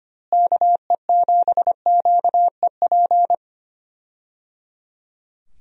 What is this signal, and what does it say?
Morse code, heard as a keyed tone, about 25 wpm, 700 Hz. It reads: KE7QEP
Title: USA Call Signs - 6 Characters - Repeated 3x - 25wpm